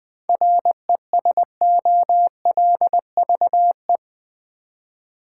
Morse code, heard as a keyed tone, about 20 wpm, 700 Hz. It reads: RESOLVE
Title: Seven Letter Words - 20wpm